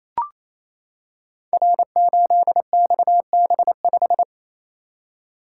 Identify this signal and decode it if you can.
Morse code, heard as a keyed tone, about 28 wpm, 700 Hz. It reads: R8XB5
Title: Code Groups - 5 Letter:Numbers - 28wpm